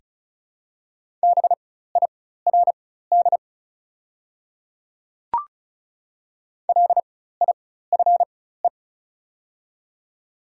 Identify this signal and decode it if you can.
Morse code, heard as a keyed tone, about 36 wpm, spaced as for 16 wpm, 690 Hz. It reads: BIRD LIFE
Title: 4x Character Spacing - Top 500 Words - 35wpm